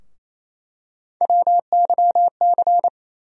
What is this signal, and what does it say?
Morse code, heard as a keyed tone, about 28 wpm, 700 Hz. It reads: WYC
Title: USA Call Sign Postfixes - 3 Letters 28wpm